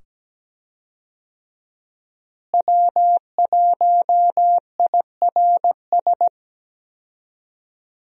Morse code, heard as a keyed tone, about 17 wpm, 700 Hz. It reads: W1IRS
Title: USA Call Signs - 5 Characters - Repeated 2x - 17wpm